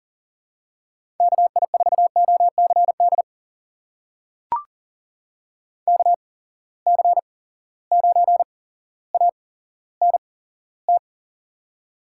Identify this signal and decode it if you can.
Morse code, heard as a keyed tone, about 39 wpm, spaced as for 17 wpm, 700 Hz. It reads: KI4OCD KC9ANT
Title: USA Call Signs - 6 Characters - 8x Character Spacing - 40wpm